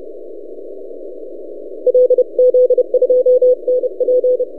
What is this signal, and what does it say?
Morse code, heard as a keyed tone, about 31 wpm, 500 Hz. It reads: LZ2NP
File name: posts / c5x-30cw- -0600z